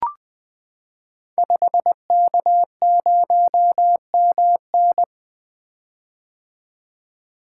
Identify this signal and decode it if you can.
Morse code, heard as a keyed tone, about 20 wpm, 700 Hz. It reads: 5K0MN
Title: Code Groups - 5 Letter:Numbers - 20wpm